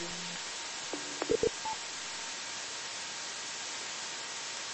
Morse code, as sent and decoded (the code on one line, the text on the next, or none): none